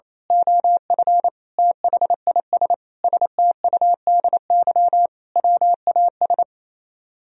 Morse code, heard as a keyed tone, about 27 wpm, 700 Hz. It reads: OF THIS STUDY WAS
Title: Speed-Racing - Sets of 5 Words 28wpm